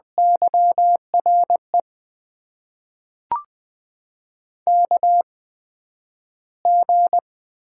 Morse code, heard as a keyed tone, about 20 wpm, spaced as for 11 wpm, 700 Hz. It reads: YRE KG
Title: USA Call Signs - 6 Characters - 8x Character Spacing - 20wpm